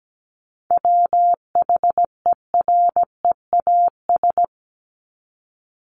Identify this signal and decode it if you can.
Morse code, heard as a keyed tone, about 17 wpm, 700 Hz. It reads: WHEREAS
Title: Top 1501-1750 Words 17wpm